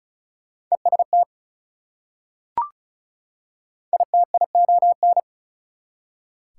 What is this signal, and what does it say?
Morse code, heard as a keyed tone, about 34 wpm, 700 Hz. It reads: EST ITION
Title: Prefixes and Suffixes 35wpm